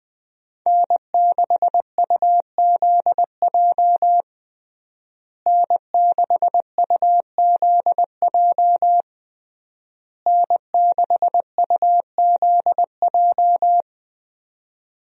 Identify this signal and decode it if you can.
Morse code, heard as a keyed tone, about 21 wpm, 700 Hz. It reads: N6UZJ N6UZJ N6UZJ
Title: USA Call Signs - 5 Characters - Repeated 3x - 20wpm